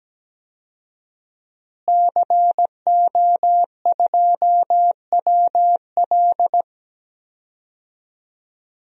Morse code, heard as a keyed tone, about 17 wpm, 700 Hz. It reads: CO2WL
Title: DXCC-100 Call Signs - 5 Characters - Repeated 3x - 17wpm